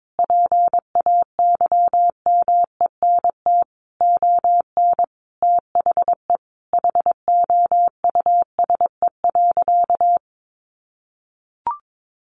Morse code, heard as a keyed tone, about 22 wpm, 700 Hz. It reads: PAYMENT ON THE HOUSE.
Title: Sentences from Top 1400 Words 22wpm